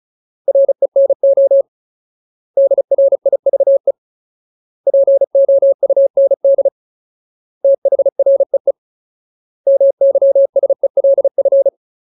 Morse code, heard as a keyed tone, about 34 wpm, 550 Hz. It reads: RENO DRIVE POUND THREE MYSELF
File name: Ultimate Word List / 35WPM Ultimate